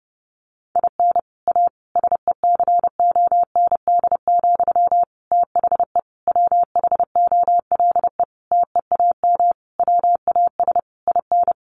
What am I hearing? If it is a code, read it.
Morse code, heard as a keyed tone, about 30 wpm, 700 Hz. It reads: IN A SECOND, THE WHOLE TEAM WAS IN
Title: Sentences from Top 1500 Words 30wpm